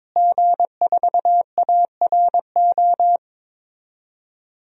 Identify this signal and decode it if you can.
Morse code, heard as a keyed tone, about 22 wpm, 700 Hz. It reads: G4ARO
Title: DXCC-100 Call Signs - 5 Characters - Repeated 2x - 22wpm